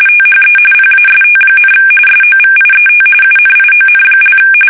RTTY, about 45.45 baud. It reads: THE PAST, SOVIET UNION)
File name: fsk850 / teletype-RTTY